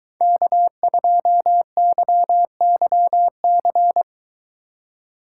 Morse code, heard as a keyed tone, about 22 wpm, 700 Hz. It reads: K2YYC